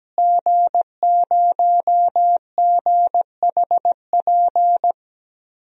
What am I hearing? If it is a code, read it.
Morse code, heard as a keyed tone, about 17 wpm, 700 Hz. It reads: G0GHP
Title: DXCC-100 Call Signs - 5 Characters - Repeated 2x - 17wpm